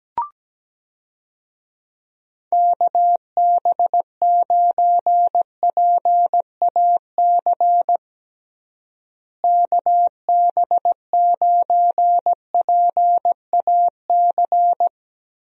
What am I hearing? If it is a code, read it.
Morse code, heard as a keyed tone, about 17 wpm, 700 Hz. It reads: KB9PAC KB9PAC
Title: USA Call Signs - 6 Characters - Repeated 2x - 17wpm